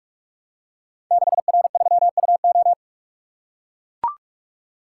Morse code, heard as a keyed tone, about 44 wpm, 700 Hz. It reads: 6R3UO
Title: Code Groups - 5 Letter:Numbers - 45wpm